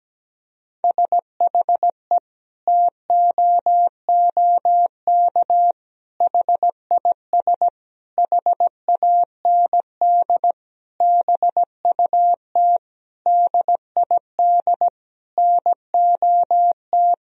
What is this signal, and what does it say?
Morse code, heard as a keyed tone, about 17 wpm, 700 Hz. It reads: SHE TOOK HIS HAND BUT DID NOT
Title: Sentences from Top 1000 Words - Review Part 1 - 17wpm